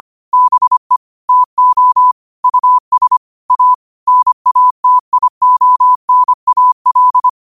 Morse code, heard as a keyed tone, about 25 wpm, 1000 Hz. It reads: BE TO US A NATIONAL